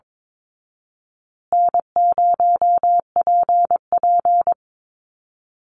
Morse code, heard as a keyed tone, about 22 wpm, 700 Hz. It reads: N0PP
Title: Easy Call Signs Repeated 3x - 22wpm